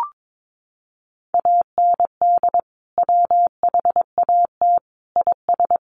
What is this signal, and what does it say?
Morse code, heard as a keyed tone, about 22 wpm, 700 Hz. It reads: AND WHAT IS
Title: Sentences from Top 1500 Words 22wpm